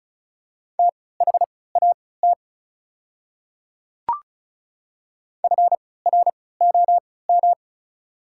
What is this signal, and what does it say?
Morse code, heard as a keyed tone, about 36 wpm, spaced as for 20 wpm, 700 Hz. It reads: THAT FROM
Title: 3x Character Spacing - Top 200 Words - 35wpm